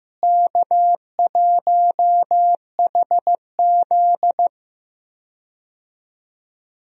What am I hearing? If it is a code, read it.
Morse code, heard as a keyed tone, about 15 wpm, 700 Hz. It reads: K1HZ
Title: USA Call Signs - 4 Characters - Repeated 2x - 15wpm